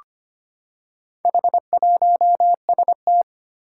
Morse code, heard as a keyed tone, about 25 wpm, 700 Hz. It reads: H1ST
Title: Four Challenging Letter-Number Combinations 25wpm